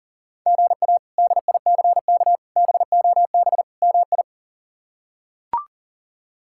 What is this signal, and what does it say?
Morse code, heard as a keyed tone, about 40 wpm, 700 Hz. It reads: GA DICK BOB MI
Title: K1USN SST QSO Practice - 40wpm